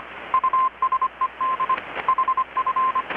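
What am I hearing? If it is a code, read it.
Morse code, heard as a keyed tone, about 25 wpm, 1040 Hz, with radio noise. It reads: USED HF
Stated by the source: Morse from KPH's Night of Nights XX maritime radiotelegraphy event, received in Chiapas, Mexico about 3,800 km away on a Kenwood R-600 with a grounded 10 m long-wire antenna.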